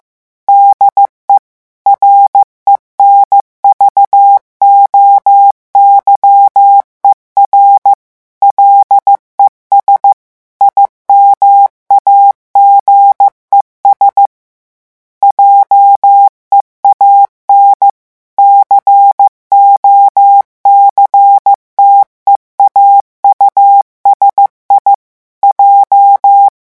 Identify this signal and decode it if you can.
Morse code, heard as a keyed tone, about 15 wpm, 800 Hz. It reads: DE RENVOYER LES IMAGES JEAN COCTEAU SI J
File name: morse mp3 / revision9 citations humoristiques vitesse 14 mots minute